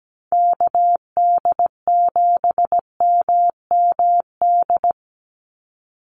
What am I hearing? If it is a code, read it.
Morse code, heard as a keyed tone, about 17 wpm, 700 Hz. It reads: KD7MMD
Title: USA Call Signs 17wpm